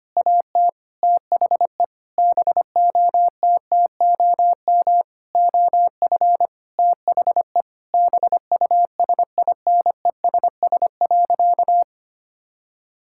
Morse code, heard as a keyed tone, about 25 wpm, 700 Hz. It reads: AT THE BOTTOM OF THE BUSINESS.
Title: Sentences from Top 1000 Words - Review Part 6 - 25wpm